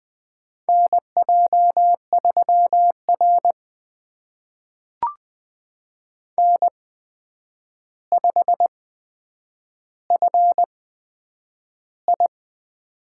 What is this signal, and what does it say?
Morse code, heard as a keyed tone, about 20 wpm, spaced as for 9 wpm, 700 Hz. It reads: NJ3R N5FI
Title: USA Call Signs - 4 Characters - 8x Character Spacing - 20wpm